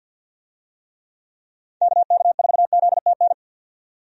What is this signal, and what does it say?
Morse code, heard as a keyed tone, about 50 wpm, 700 Hz. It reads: KK4ZTN
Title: USA Call Signs - 6 Characters - 50wpm